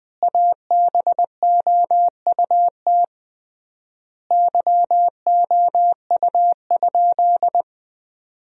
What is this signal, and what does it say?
Morse code, heard as a keyed tone, about 20 wpm, 700 Hz. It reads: ABOUT YOU?
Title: 3x Word-Spacing - Sentences from Top 100 Words - 20wpm